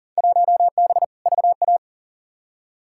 Morse code, heard as a keyed tone, about 40 wpm, 700 Hz. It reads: 1B VA
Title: Field Day Exchanges Encore - 40wpm